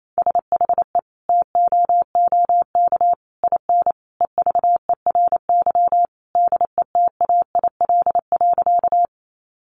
Morse code, heard as a keyed tone, about 28 wpm, 700 Hz. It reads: SHE TOOK IN EVERY DETAIL.
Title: Sentences from Top 1500 Words 28wpm